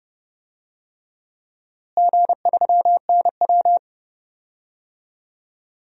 Morse code, heard as a keyed tone, about 32 wpm, 700 Hz. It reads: G3NW
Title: DXCC-100 Call Signs - 4 Characters - Repeated 3x - 30wpm